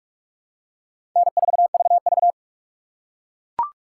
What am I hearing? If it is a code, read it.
Morse code, heard as a keyed tone, about 42 wpm, 700 Hz. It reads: N4VV
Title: USA Call Signs - 4 Characters - 4x Character Spacing - 45wpm